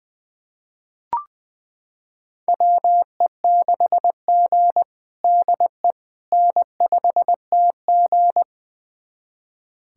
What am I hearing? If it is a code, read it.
Morse code, heard as a keyed tone, about 20 wpm, 700 Hz. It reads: WE6G DE N5TG
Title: USA Callsign DE Callsign - 4 Characters - 20wpm